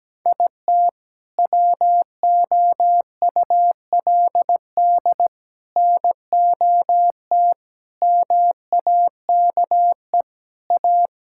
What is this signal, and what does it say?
Morse code, heard as a keyed tone, about 17 wpm, 700 Hz. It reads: IT WOULD NOT MAKE A
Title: Sentences from Top 1000 Words - Review Part 5 - 17wpm